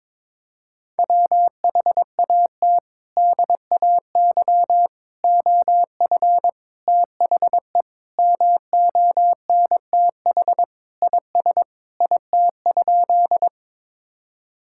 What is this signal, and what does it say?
Morse code, heard as a keyed tone, about 21 wpm, 700 Hz. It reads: WHAT DAY OF THE MONTH IS IT?
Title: Sentences from Top 1000 Words - Review Part 5 - 22wpm